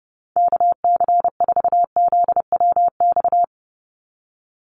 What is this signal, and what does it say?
Morse code, heard as a keyed tone, about 30 wpm, 700 Hz. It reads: KC4ZWX
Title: USA Call Signs 30wpm